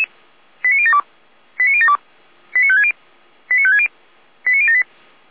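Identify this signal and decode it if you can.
Five-tone selective call (ZVEI), heard as quick runs of five tones, tones of about 70 ms. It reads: 89072 89072 89560 89560 89078
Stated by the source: Received on an SDR.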